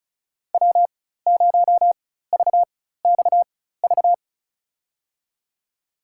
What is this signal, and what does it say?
Morse code, heard as a keyed tone, about 34 wpm, spaced as for 16 wpm, 700 Hz. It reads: W0VXV
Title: USA Call Signs - 5 Characters - 4x Character Spacing - 35wpm